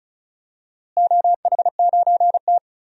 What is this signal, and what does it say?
Morse code, heard as a keyed tone, about 36 wpm, 700 Hz. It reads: OH9T
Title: DXCC-100 Call Signs - 4 Characters - Repeated 2x - 35wpm